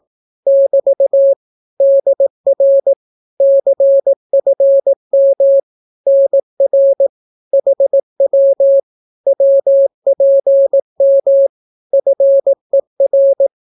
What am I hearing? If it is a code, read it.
Morse code, heard as a keyed tone, about 18 wpm, 550 Hz. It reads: <BT> DR CFM NR HW WPM FER
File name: naukapodstaw / skroty / skroty